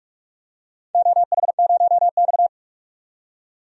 Morse code, heard as a keyed tone, about 46 wpm, 690 Hz. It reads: OH0X
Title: DXCC-100 Call Signs - 4 Characters - 45wpm